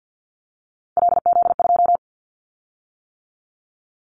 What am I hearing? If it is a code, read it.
Morse code, heard as a keyed tone, about 50 wpm, 700 Hz. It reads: LZ2
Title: Callsign Prefixes for Top 50 DXCC - 50wpm